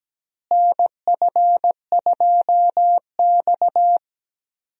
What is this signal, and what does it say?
Morse code, heard as a keyed tone, about 17 wpm, 700 Hz. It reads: NF2X
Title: USA Call Signs - 4 Characters - Repeated 2x - 17wpm